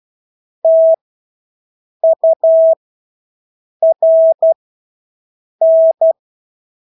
Morse code, heard as a keyed tone, about 12 wpm, spaced as for 6 wpm, 650 Hz. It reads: TURN
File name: Engelske ord 12-6 wpm